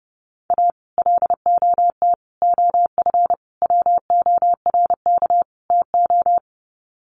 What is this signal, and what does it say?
Morse code, heard as a keyed tone, about 30 wpm, 700 Hz. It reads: A LOT OF WORK TO
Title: Sets of 5 Words Encore2 30wpm